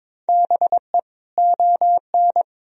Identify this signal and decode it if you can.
Morse code, heard as a keyed tone, about 22 wpm, 700 Hz. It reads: BE ON
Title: Sentences from Top 1000 Words - Review Part 1 - 22wpm